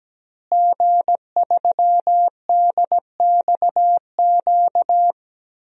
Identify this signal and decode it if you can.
Morse code, heard as a keyed tone, about 17 wpm, 700 Hz. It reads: G3DXQ
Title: DXCC-100 Call Signs - 5 Characters - Repeated 3x - 17wpm